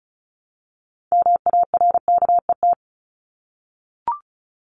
Morse code, heard as a keyed tone, about 34 wpm, 700 Hz. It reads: MARKET
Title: Top 501-750 Words 35wpm